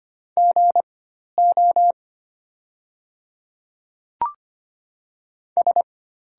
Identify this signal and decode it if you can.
Morse code, heard as a keyed tone, about 25 wpm, spaced as for 12 wpm, 690 Hz. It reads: GO S
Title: 4x Character Spacing - Top 500 Words - 25wpm